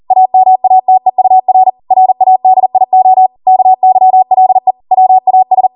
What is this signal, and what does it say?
Morse code, heard as a keyed tone, about 40 wpm, 750 Hz. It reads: AMATEUR RADIO KYLE WAS